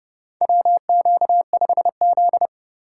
Morse code, heard as a keyed tone, about 30 wpm, 700 Hz. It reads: WQ5Z
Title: USA Call Signs - 3x Character Spacing - 30wpm